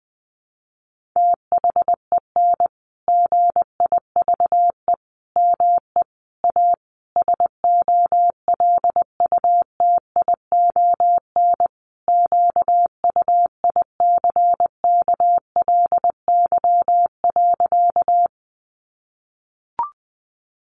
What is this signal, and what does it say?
Morse code, heard as a keyed tone, about 20 wpm, 700 Hz. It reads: THEN GIVE ME A SOLUTION QUICKLY.
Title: Sentences from Top 1300 Words 20wpm